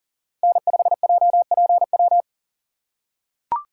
Morse code, heard as a keyed tone, about 37 wpm, 700 Hz. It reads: N5JPW
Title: USA Call Signs - 5 Characters - 3x Character Spacing - 40wpm